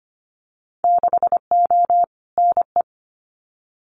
Morse code, heard as a keyed tone, about 25 wpm, 700 Hz. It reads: 6O NE
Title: Winter Field Day Exchanges Encore 25wpm